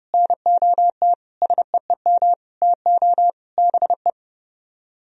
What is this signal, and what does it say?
Morse code, heard as a keyed tone, about 30 wpm, 700 Hz. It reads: NOT SEEM TO BE